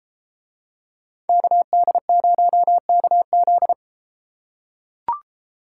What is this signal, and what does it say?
Morse code, heard as a keyed tone, about 30 wpm, 700 Hz. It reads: KD0KZ